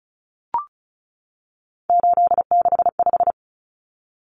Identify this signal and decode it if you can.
Morse code, heard as a keyed tone, about 34 wpm, 700 Hz. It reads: 865
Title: Common Cut Numbers - 3-Digit 35wpm